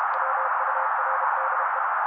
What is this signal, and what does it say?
Morse code, heard as a keyed tone, about 32 wpm, 530 Hz. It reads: WURDE
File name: daily / 31wpm noise